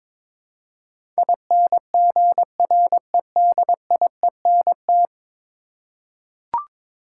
Morse code, heard as a keyed tone, about 22 wpm, 700 Hz. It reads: INGREDIENT
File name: practice / 22wpm / 10 Letter Words - First 3 Letters Given - 22wpm